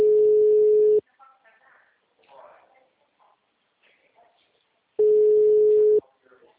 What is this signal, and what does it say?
Ringback tone (Europe, 425 Hz), with speech.